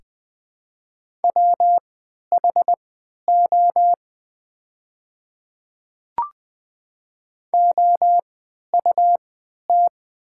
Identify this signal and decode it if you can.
Morse code, heard as a keyed tone, about 20 wpm, spaced as for 11 wpm, 700 Hz. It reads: WHO OUT
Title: 3x Character Spacing - Top 300 Words - 20wpm